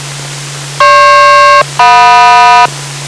Aircraft SELCAL, heard as two pairs of tones. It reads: GP-JM